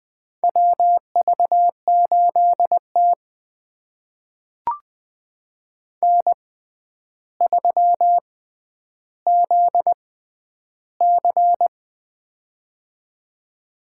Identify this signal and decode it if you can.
Morse code, heard as a keyed tone, about 21 wpm, spaced as for 10 wpm, 700 Hz. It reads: WV8T N3ZC
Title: USA Call Signs - 4 Characters - 6x Character Spacing - 20wpm